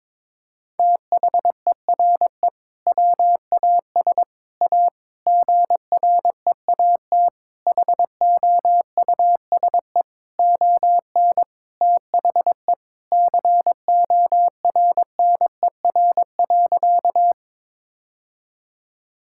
Morse code, heard as a keyed tone, about 22 wpm, 700 Hz. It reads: THERE WAS A GREAT HOUSE ON THE CORNER.
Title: Sentences from Top 1600 Words 22wpm